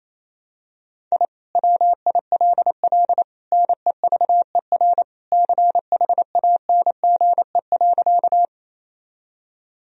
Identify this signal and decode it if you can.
Morse code, heard as a keyed tone, about 28 wpm, 700 Hz. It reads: I WILL NEVER CHANGE.
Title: Sentences from Top 1000 Words - Review Part 1 - 28wpm